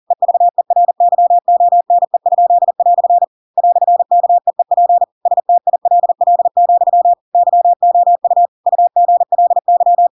Morse code, heard as a keyed tone, about 40 wpm, 700 Hz. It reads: EVERYONE?" "KEEP STILL, YOU UGLY
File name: Pinocchio / Pinocchio ch4 40 40 ik1hge